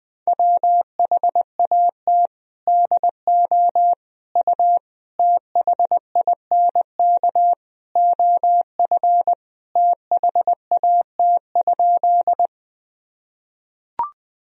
Morse code, heard as a keyed tone, about 20 wpm, 700 Hz. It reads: WHAT DO U THINK OF THAT?
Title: Sentences from Top 200 Words and Abbreviations 20wpm